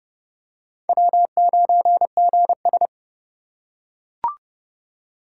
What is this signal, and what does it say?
Morse code, heard as a keyed tone, about 28 wpm, 700 Hz. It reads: W9GS